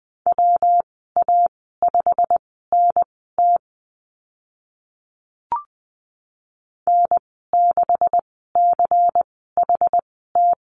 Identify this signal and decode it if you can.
Morse code, heard as a keyed tone, about 20 wpm, spaced as for 14 wpm, 700 Hz. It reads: WA5NT N6CHT
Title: 167 Lesson 27g - Number 7 - Call Signs 20wpm